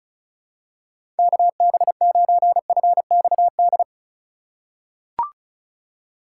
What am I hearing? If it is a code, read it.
Morse code, heard as a keyed tone, about 36 wpm, 700 Hz. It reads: KB9FXD
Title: USA Call Signs - 6 Characters - 8x Character Spacing - 35wpm